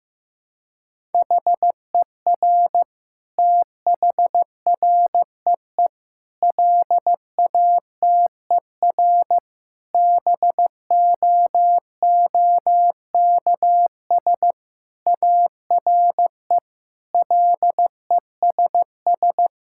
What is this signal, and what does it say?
Morse code, heard as a keyed tone, about 15 wpm, 700 Hz. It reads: HER THREE LATER BOOKS ARE LESS
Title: Sentences from Top 1000 Words - Review Part 6 - 15wpm